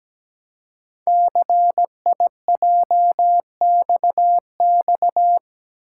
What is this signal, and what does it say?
Morse code, heard as a keyed tone, about 17 wpm, 700 Hz. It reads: CIJXX
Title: Code Groups - 5 Letters - 17wpm